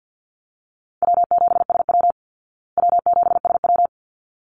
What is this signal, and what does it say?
Morse code, heard as a keyed tone, about 50 wpm, 700 Hz. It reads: W7SW W7SW
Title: Easy Call Signs Repeated 2x - 50wpm